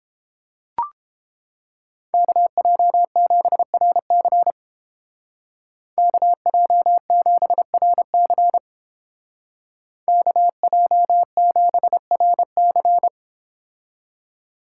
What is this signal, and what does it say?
Morse code, heard as a keyed tone, about 29 wpm, 700 Hz. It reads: KJ7RC KJ7RC KJ7RC